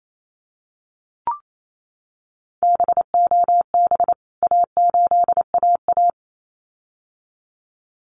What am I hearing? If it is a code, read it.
Morse code, heard as a keyed tone, about 28 wpm, 700 Hz. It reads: BOB A8AA
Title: CWT Exchange 28wpm